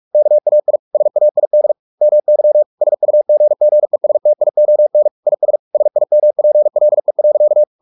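Morse code, heard as a keyed tone, about 44 wpm, 600 Hz. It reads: KAI SAID MY SUGGESTION IS SIMPLE.
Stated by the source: Morse practice at 45 wpm, the novel Never by Ken Follett in English, from CW med Gnister.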